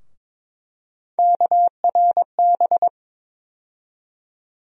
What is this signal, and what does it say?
Morse code, heard as a keyed tone, about 22 wpm, 700 Hz. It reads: KRB
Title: USA Call Sign Postfixes - 3 Letters 22wpm